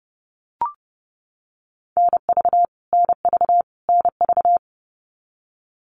Morse code, heard as a keyed tone, about 30 wpm, 700 Hz. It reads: NV NV NV
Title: US States Repeated 3x 30wpm